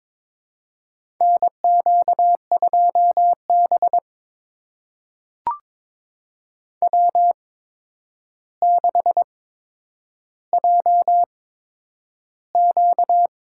Morse code, heard as a keyed tone, about 22 wpm, spaced as for 9 wpm, 700 Hz. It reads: NQ2B W6JQ
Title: USA Call Signs - 8x Character Spacing - 22wpm